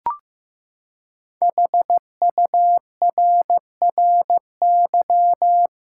Morse code, heard as a keyed tone, about 15 wpm, 700 Hz. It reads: HURRY
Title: Five Letter Words - 15wpm